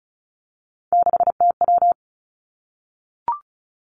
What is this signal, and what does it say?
Morse code, heard as a keyed tone, about 36 wpm, 700 Hz. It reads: 6TW
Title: Three Challenging Letter-Number Combinations 35wpm